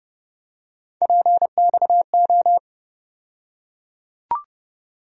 Morse code, heard as a keyed tone, about 30 wpm, 700 Hz. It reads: PXO
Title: USA Call Sign Postfixes - 3 Letters 30wpm